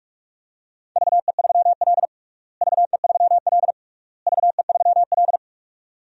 Morse code, heard as a keyed tone, about 46 wpm, 700 Hz. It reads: VE3L VE3L VE3L
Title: DXCC-100 Call Signs - 4 Characters - Repeated 3x - 45wpm